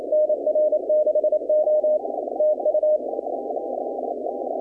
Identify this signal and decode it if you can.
Morse code, heard as a keyed tone, about 28 wpm, 610 Hz, with noise. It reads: NR6O TU
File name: contests / 2019-cw-wpx-kt5j-k5pi- -2330 0-20